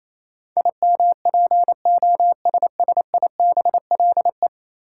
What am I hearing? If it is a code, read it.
Morse code, heard as a keyed tone, about 28 wpm, 700 Hz. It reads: IMPOSSIBLE
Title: Sentences from Top 1100 Words - New Word Drill - 28wpm